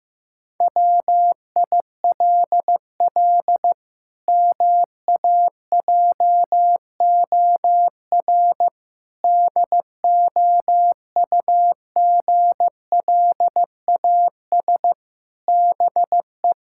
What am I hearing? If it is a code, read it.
Morse code, heard as a keyed tone, about 16 wpm, 700 Hz. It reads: WILL MAJOR DOUGLAS BE
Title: Pangram Challenge v2 - 15wpm